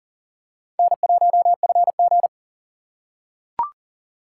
Morse code, heard as a keyed tone, about 40 wpm, 700 Hz. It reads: N1FG
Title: USA Call Signs - 4 Characters - 3x Character Spacing - 40wpm